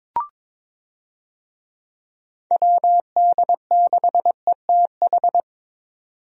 Morse code, heard as a keyed tone, about 22 wpm, 700 Hz. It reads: WD6ETH
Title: USA Call Signs - 6 Characters - 22wpm